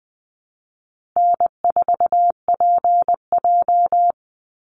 Morse code, heard as a keyed tone, about 19 wpm, 700 Hz. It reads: N4PJ
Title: Easy Call Signs Repeated 2x - 20wpm